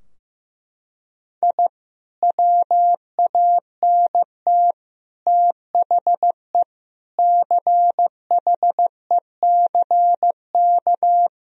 Morse code, heard as a keyed tone, about 15 wpm, 700 Hz. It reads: I WANT THE CHECK
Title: Sentences from Top 1000 Words - Review Part 3 - 15wpm